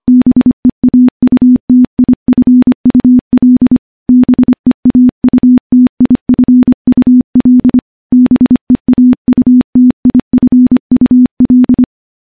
Morse code, heard as a keyed tone, about 25 wpm, 250 Hz. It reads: BEAUTIFUL BEAUTIFUL BEAUTIFUL
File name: morse-training-25WPM